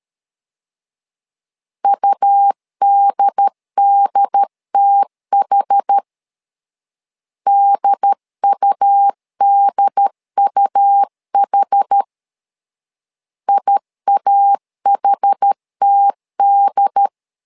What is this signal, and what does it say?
Morse code, heard as a keyed tone, about 13 wpm, 800 Hz. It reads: UDDTH DUDUH IAHTD
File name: LEZIONI / 2 UHDN / 2 UHD 12wpm 1540